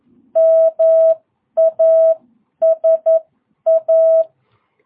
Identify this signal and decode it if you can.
Morse code, heard as a keyed tone, about 11 wpm, spaced as for 9 wpm, 650 Hz. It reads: MASA
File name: jh8cbh / 2022cw